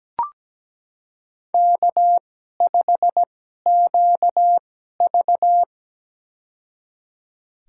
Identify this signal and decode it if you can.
Morse code, heard as a keyed tone, about 17 wpm, spaced as for 12 wpm, 700 Hz. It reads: K5QV
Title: USA Call Signs - 4 Characters - 2x Character Spacing - 17wpm